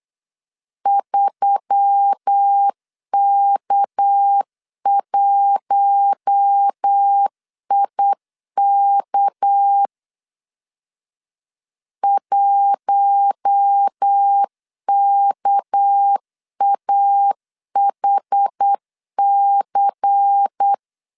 Morse code, heard as a keyed tone, about 8 wpm, 800 Hz. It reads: 3K1IK 1KAHC
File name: LEZIONI / 6 PKX0 / 6 PK 8wpm 3003